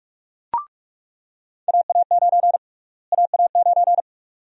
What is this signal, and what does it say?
Morse code, heard as a keyed tone, about 46 wpm, 700 Hz. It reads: AA9 AA9
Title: USA Call Sign Prefixes - Repeated 2x - 45wpm